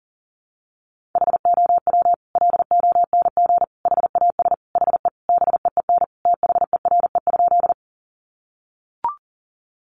Morse code, heard as a keyed tone, about 40 wpm, 700 Hz. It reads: HOW LONG HAS HE BEEN THERE?
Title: Sentences from Top 100 Words 40wpm